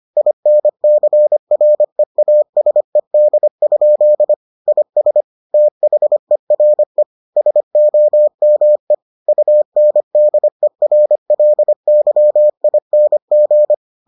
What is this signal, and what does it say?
Morse code, heard as a keyed tone, about 25 wpm, 600 Hz. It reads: INCREASED? IS THERE SOME UNDERLYING